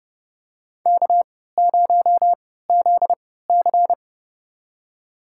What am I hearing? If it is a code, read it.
Morse code, heard as a keyed tone, about 30 wpm, spaced as for 17 wpm, 700 Hz. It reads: K0ZC
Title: USA Call Signs - 3x Character Spacing - 30wpm